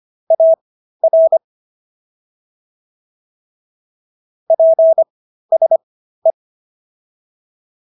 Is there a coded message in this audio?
Yes: Morse code from 0.3 to 6.3 s, heard as a keyed tone, about 25 wpm, spaced as for 13 wpm, 650 Hz. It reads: AR PSE